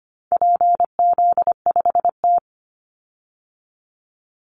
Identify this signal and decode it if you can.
Morse code, heard as a keyed tone, about 25 wpm, 700 Hz. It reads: PZ5T
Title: Call Signs Repeated 2x - 25wpm